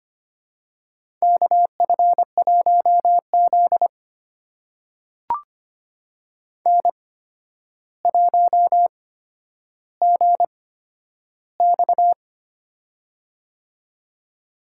Morse code, heard as a keyed tone, about 25 wpm, spaced as for 11 wpm, 700 Hz. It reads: KF1Z N1GX
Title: USA Call Signs - 4 Characters - 8x Character Spacing - 25wpm